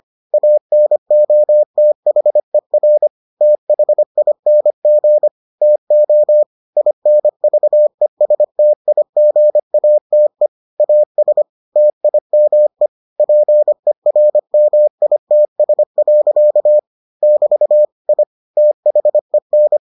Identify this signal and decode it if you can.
Morse code, heard as a keyed tone, about 25 wpm, 600 Hz. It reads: ANOTHER THING TO INVESTIGATE AS TIME PERMITS. <BT> I THEN